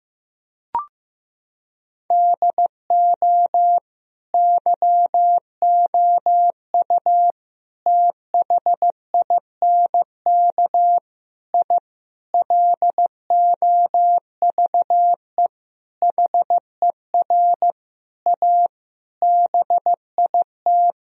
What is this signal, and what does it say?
Morse code, heard as a keyed tone, about 16 wpm, 700 Hz. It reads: DO YOU THINK I LOVE HER A BIT
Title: Sentences from Top 1000 Words - Review Part 2 - 15wpm